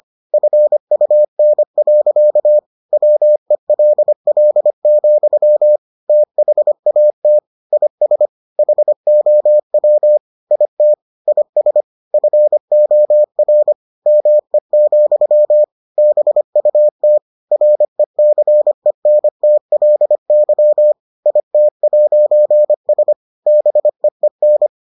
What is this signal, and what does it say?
Morse code, heard as a keyed tone, about 25 wpm, 600 Hz. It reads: FUN. WELL, THAT IS HOW IT IS FOR ME, BUT RECENTLY IT'S BEEN